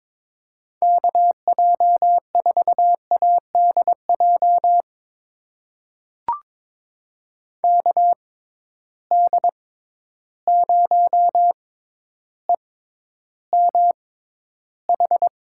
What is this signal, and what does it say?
Morse code, heard as a keyed tone, about 22 wpm, spaced as for 11 wpm, 700 Hz. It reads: KJ4ADJ KD0EMH
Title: USA Call Signs - 6 Characters - 6x Character Spacing - 22wpm